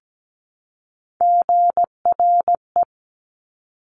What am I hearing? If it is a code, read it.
Morse code, heard as a keyed tone, about 17 wpm, 700 Hz. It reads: GRE
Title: Top Three Letter Combinations in Words 17wpm